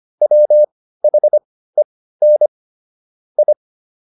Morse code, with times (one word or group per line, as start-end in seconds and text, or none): 0.2-2.5 WHEN
3.4-3.5 I